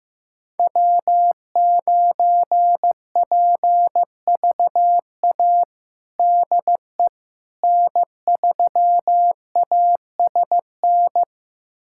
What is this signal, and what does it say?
Morse code, heard as a keyed tone, about 15 wpm, 700 Hz. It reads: W9PVA DE N3ASN
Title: USA Callsign DE Callsign - 5 Characters - 15wpm